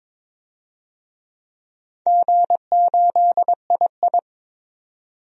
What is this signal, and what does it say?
Morse code, heard as a keyed tone, about 22 wpm, 690 Hz. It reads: G8II
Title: DXCC-100 Call Signs - 4 Characters - 22wpm